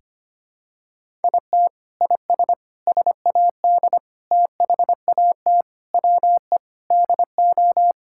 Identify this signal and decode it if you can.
Morse code, heard as a keyed tone, about 25 wpm, 700 Hz. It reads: IT IS SAD THAT WE DO
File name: practice / 25wpm / Sentences with 4-Word Phrases - 25wpm